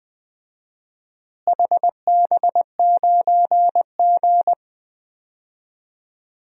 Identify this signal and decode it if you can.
Morse code, heard as a keyed tone, about 20 wpm, 700 Hz. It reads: HB9G
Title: DXCC-100 Call Signs - 4 Characters - Repeated 2x - 20wpm